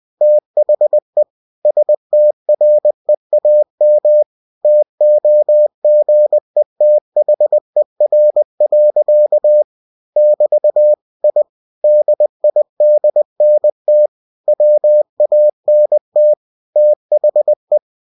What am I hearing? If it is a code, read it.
Morse code, heard as a keyed tone, about 20 wpm, 600 Hz. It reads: THE STREAM TOGETHER. <BT> I DIDNT WANT THE
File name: cw-books / winnie-the-pooh / winnie-20wpm-page-0084